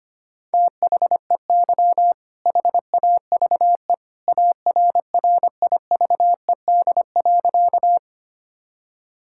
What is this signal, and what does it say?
Morse code, heard as a keyed tone, about 25 wpm, 700 Hz. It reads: THEY HAVE ARRIVED.
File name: practice / 25wpm / Sentences with 2-Word Phrases - 25wpm